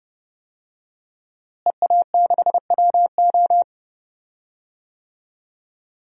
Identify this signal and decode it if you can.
Morse code, heard as a keyed tone, about 32 wpm, 700 Hz. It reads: EA6WO
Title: DXCC-100 Call Signs - 5 Characters - Repeated 3x - 30wpm